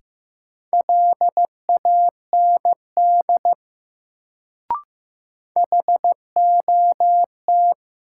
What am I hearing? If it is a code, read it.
Morse code, heard as a keyed tone, about 15 wpm, 700 Hz. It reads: LAND HOT
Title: Sentences from Top 500 Words - New Word Drill - 15wpm